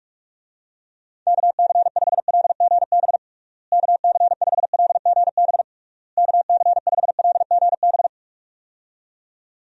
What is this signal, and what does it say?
Morse code, heard as a keyed tone, about 46 wpm, 700 Hz. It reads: KC5LGB KC5LGB KC5LGB
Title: USA Call Signs - 6 Characters - Repeated 3x - 45wpm